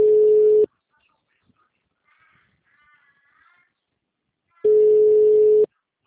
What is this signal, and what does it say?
Ringback tone (Europe, 425 Hz).